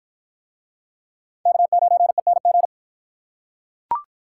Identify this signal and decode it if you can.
Morse code, heard as a keyed tone, about 46 wpm, 700 Hz. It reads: K9ENG